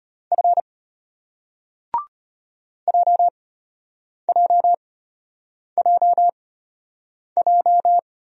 Morse code, heard as a keyed tone, about 34 wpm, spaced as for 8 wpm, 700 Hz. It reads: F JJJJ